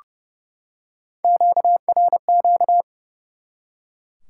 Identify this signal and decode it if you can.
Morse code, heard as a keyed tone, about 30 wpm, 700 Hz. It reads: QRQ
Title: Top 90 QSO Elements 30wpm